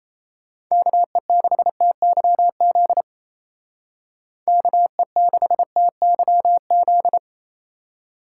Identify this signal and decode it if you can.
Morse code, heard as a keyed tone, about 36 wpm, 700 Hz. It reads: KE6TYZ KE6TYZ